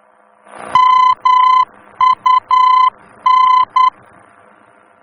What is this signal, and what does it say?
Morse code, heard as a keyed tone, about 10 wpm, 1020 Hz. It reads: MUN
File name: germany / mx / MUN-D-VOR